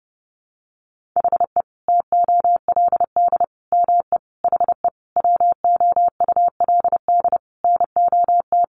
Morse code, heard as a keyed tone, about 30 wpm, 700 Hz. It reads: HE TOLD ME HE WOULD NOT
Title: Sentences from Top 1300 Words 30wpm